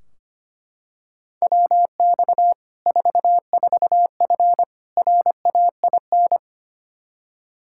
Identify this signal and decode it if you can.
Morse code, heard as a keyed tone, about 25 wpm, 700 Hz. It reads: WX 44F RAIN
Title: Practice QSO - Exchange 2 - Info - 25wpm